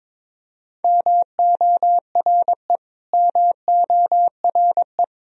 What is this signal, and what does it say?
Morse code, heard as a keyed tone, about 21 wpm, 700 Hz. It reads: MORE MORE
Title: Top 100 Words Repeated 22wpm